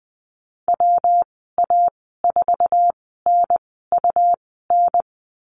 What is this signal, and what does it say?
Morse code, heard as a keyed tone, about 20 wpm, spaced as for 15 wpm, 700 Hz. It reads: WA4NUN
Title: 186 Lesson 30g - Letter b - Call Signs 20wpm